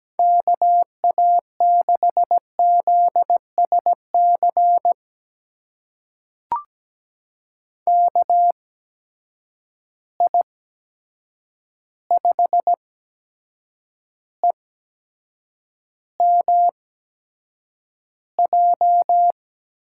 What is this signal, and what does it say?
Morse code, heard as a keyed tone, about 17 wpm, spaced as for 7 wpm, 700 Hz. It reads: KA6ZSC KI5EMJ
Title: USA Call Signs - 6 Characters - 8x Character Spacing - 17wpm